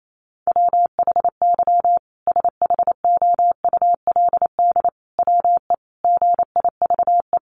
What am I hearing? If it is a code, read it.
Morse code, heard as a keyed tone, about 28 wpm, 700 Hz. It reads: WHY SHOULD WE GIVE
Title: Sentences from Top 700 Words 28wpm